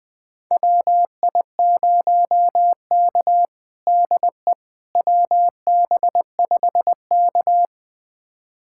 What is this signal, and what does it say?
Morse code, heard as a keyed tone, about 20 wpm, 700 Hz. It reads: WI0K DE WB5K
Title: USA Callsign DE Callsign - 4 Characters - 20wpm